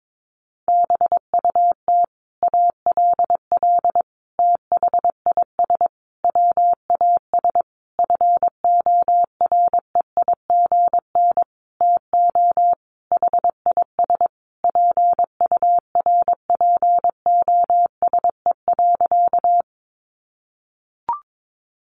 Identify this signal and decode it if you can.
Morse code, heard as a keyed tone, about 22 wpm, 700 Hz. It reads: BUT ALL THIS WAS FOREIGN TO HIS PURPOSE.
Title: Sentences from Top 1400 Words 22wpm